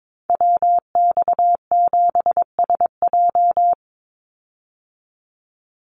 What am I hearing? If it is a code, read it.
Morse code, heard as a keyed tone, about 22 wpm, 700 Hz. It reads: WX7SJ
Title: Call Signs Repeated 2x - 22wpm